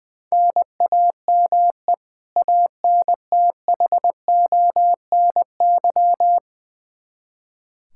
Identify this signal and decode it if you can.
Morse code, heard as a keyed tone, about 20 wpm, 700 Hz. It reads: NAME ANTHONY
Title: K1USN SST QSO Practice 20wpm